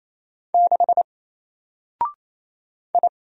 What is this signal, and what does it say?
Morse code, heard as a keyed tone, about 27 wpm, 700 Hz. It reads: 6 I
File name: Single-Letter-Number-28wpm